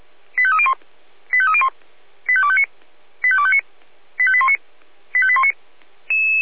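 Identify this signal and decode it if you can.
Five-tone selective call (ZVEI), heard as quick runs of five tones, tones of about 70 ms. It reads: 85301 85301 85259 85259 85819 85819
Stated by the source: ZVEI five-tone selective call received by SDR.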